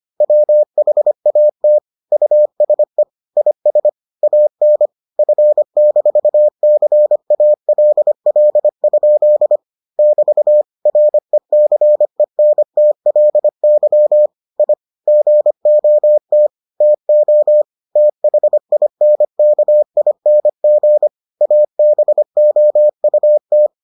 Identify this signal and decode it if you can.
Morse code, heard as a keyed tone, about 25 wpm, 600 Hz. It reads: WHAT USE IS AN F-CALL? <BT> RECENTLY I GOT TO THINKING ABOUT